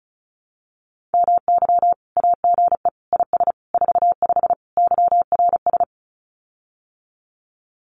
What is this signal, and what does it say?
Morse code, heard as a keyed tone, about 36 wpm, 700 Hz. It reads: MY AGE IS 45 YRS
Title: Common QSO Phrases Encore 35wpm